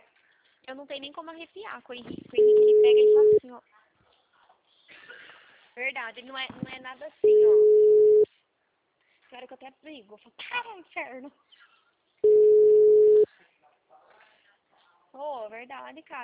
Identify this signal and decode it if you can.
Ringback tone (Europe, 425 Hz), with speech.
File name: out- -2006- - - .2639